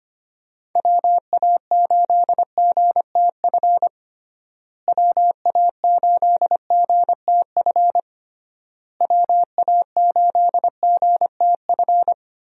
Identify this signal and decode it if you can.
Morse code, heard as a keyed tone, about 25 wpm, 700 Hz. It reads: WA8GTF WA8GTF WA8GTF
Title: USA Call Signs - 6 Characters - Repeated 3x - 25wpm